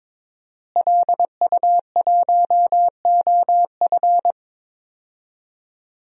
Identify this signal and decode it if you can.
Morse code, heard as a keyed tone, about 21 wpm, 700 Hz. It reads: LU1OF
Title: DXCC-100 Call Signs - 5 Characters - Repeated 3x - 22wpm